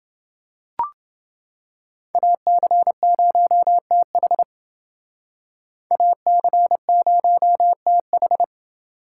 Morse code, heard as a keyed tone, about 28 wpm, 700 Hz. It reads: AC0TH AC0TH